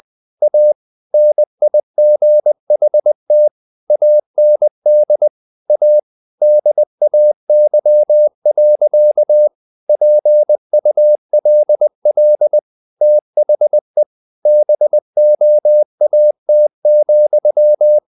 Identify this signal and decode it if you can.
Morse code, heard as a keyed tone, about 20 wpm, 600 Hz. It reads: A NIGHT AND A DAY. PULL THE BOAT,
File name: the old man and the sea part 2 20wpm